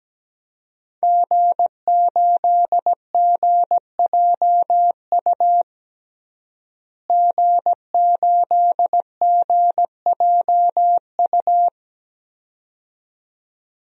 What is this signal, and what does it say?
Morse code, heard as a keyed tone, about 17 wpm, 700 Hz. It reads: G8GJU G8GJU
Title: DXCC-100 Call Signs - 5 Characters - Repeated 2x - 17wpm